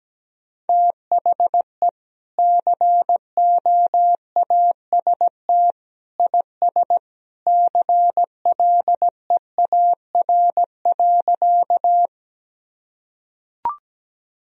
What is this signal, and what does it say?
Morse code, heard as a keyed tone, about 17 wpm, 700 Hz. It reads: THE COAST IS CLEAR.
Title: Sentences from Top 1000 Words - Review Part 4 - 17wpm